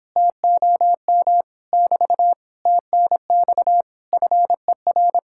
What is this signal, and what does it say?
Morse code, heard as a keyed tone, about 29 wpm, 700 Hz. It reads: TOM <BT> TNX FER